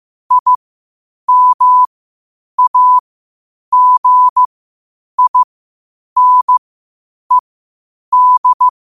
Morse code, heard as a keyed tone, about 15 wpm, spaced as for 8 wpm, 1000 Hz. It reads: IMAGINED
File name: qotdfiles / 8wpm